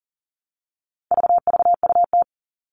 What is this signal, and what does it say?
Morse code, heard as a keyed tone, about 40 wpm, 700 Hz. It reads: VVUT
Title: All Cut Numbers - 4-Digit 40wpm